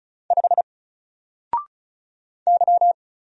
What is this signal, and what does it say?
Morse code, heard as a keyed tone, about 35 wpm, 700 Hz. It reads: 5 Y
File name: Single-Letter-Number-35wpm